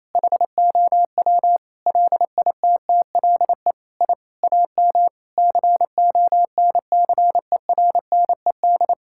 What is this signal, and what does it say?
Morse code, heard as a keyed tone, about 28 wpm, 700 Hz. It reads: HOW LITTLE I AM CONCERNED
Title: Sentences from Top 1000 Words - Review Part 6 - 28wpm